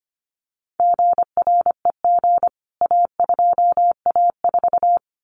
Morse code, heard as a keyed tone, about 25 wpm, 700 Hz. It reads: GREG A2A4
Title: CWT Exchange 25wpm